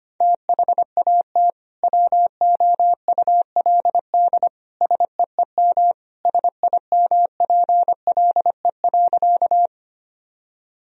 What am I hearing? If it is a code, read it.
Morse code, heard as a keyed tone, about 25 wpm, 700 Hz. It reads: THAT WOULD SEEM SIMPLE.
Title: Sentences from Top 1000 Words - Review Part 1 - 25wpm